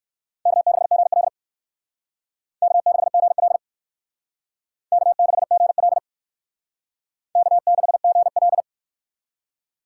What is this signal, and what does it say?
Morse code, heard as a keyed tone, about 62 wpm, 700 Hz. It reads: K6GL K6GL K6GL K6GL